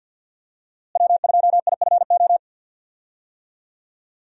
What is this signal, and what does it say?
Morse code, heard as a keyed tone, about 45 wpm, 700 Hz. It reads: W2IFO
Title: USA Call Signs - 5 Characters - Repeated 2x - 50wpm